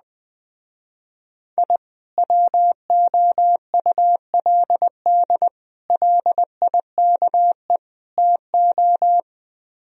Morse code, heard as a keyed tone, about 20 wpm, 700 Hz. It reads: I WOULD LIKE TO